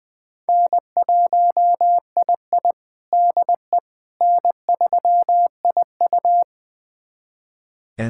Morse code, heard as a keyed tone, about 20 wpm, 700 Hz. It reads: N1II DE N3IU
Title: USA Callsign DE Callsign - 4 Characters - 20wpm